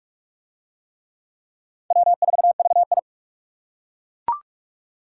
Morse code, heard as a keyed tone, about 40 wpm, 700 Hz. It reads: W4VI